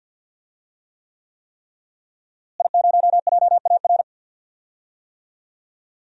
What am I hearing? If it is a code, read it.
Morse code, heard as a keyed tone, about 46 wpm, 700 Hz. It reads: I0JAR
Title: DXCC-100 Call Signs - 5 Characters - Repeated 3x - 50wpm